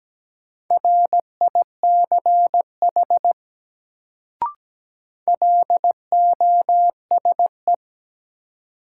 Morse code, heard as a keyed tone, about 17 wpm, 700 Hz. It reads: RICH LOSE
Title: Sentences from Top 1100 Words - New Word Drill - 17wpm